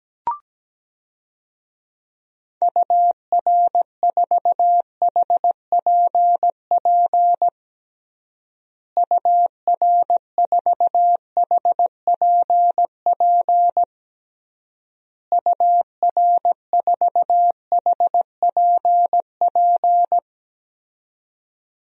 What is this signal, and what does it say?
Morse code, heard as a keyed tone, about 17 wpm, 700 Hz. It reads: UR4HPP UR4HPP UR4HPP
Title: DXCC-100 Call Signs - 6 Characters - Repeated 3x - 17wpm